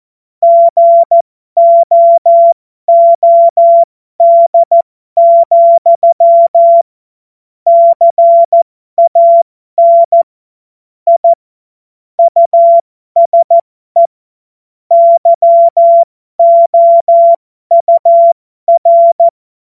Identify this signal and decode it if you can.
Morse code, heard as a keyed tone, about 14 wpm, spaced as for 12 wpm, 680 Hz. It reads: GOOD, CAN I USE YOUR
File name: Morsekurs / Witze